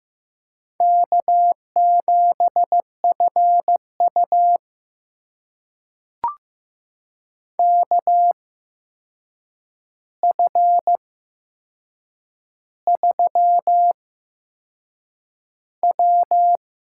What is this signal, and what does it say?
Morse code, heard as a keyed tone, about 15 wpm, spaced as for 6 wpm, 700 Hz. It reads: K7FU KF3W
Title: USA Call Signs - 4 Characters - 8x Character Spacing - 15wpm